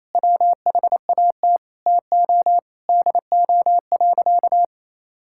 Morse code, heard as a keyed tone, about 28 wpm, 700 Hz. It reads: WHAT TO DO.
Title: Sentences from Top 1000 Words - Review Part 5 - 28wpm